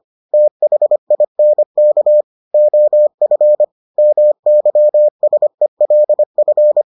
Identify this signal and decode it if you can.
Morse code, heard as a keyed tone, about 25 wpm, 600 Hz. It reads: THINK OF MYSELF